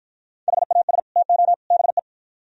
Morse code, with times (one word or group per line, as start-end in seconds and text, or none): 0.5-1.0 HAS
1.2-1.5 TO
1.7-2.0 BE